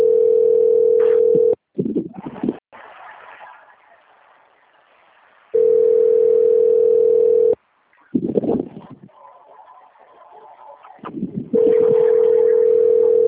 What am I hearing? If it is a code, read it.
Ringback tone (North America), 440 + 480 Hz.
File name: out- - - - - .771